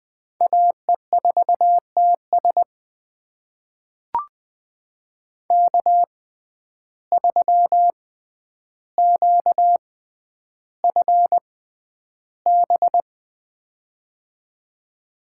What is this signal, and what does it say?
Morse code, heard as a keyed tone, about 20 wpm, spaced as for 10 wpm, 700 Hz. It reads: AE4TS K3QFB
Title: USA Call Signs - 5 Characters - 6x Character Spacing - 20wpm